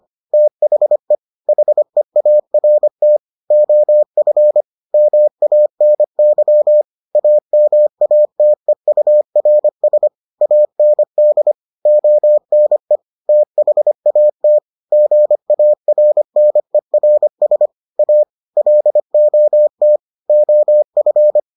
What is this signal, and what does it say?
Morse code, heard as a keyed tone, about 25 wpm, 600 Hz. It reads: THE HEART OF MANY AMATEURS AND ONE THAT GARNERS A LOT OF